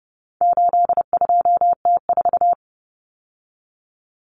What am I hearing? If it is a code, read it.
Morse code, heard as a keyed tone, about 30 wpm, 700 Hz. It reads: 82T4
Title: Common Cut Numbers - 4-Digit 30wpm